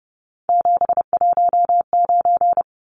Morse code, heard as a keyed tone, about 30 wpm, 700 Hz. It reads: 719
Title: Numbers - 3-Digit 30wpm